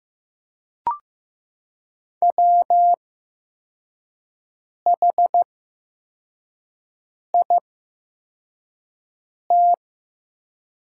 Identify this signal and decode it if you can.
Morse code, heard as a keyed tone, about 15 wpm, spaced as for 4 wpm, 700 Hz. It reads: WHIT
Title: 8x Character Spacing - Top 300 Words - 15wpm